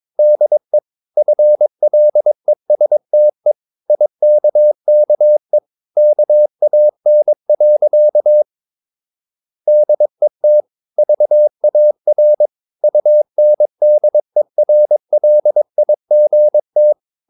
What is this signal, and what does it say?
Morse code, heard as a keyed tone, about 22 wpm, 600 Hz. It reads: DE FLESTE IKKE KAN. DET VAR UNDERLIGT
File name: Ferie QRS 22wpm